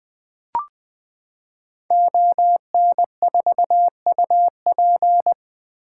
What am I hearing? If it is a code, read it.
Morse code, heard as a keyed tone, about 20 wpm, 700 Hz. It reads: ON4UP
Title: 2023 SOTA Activator Call Signs - 20wpm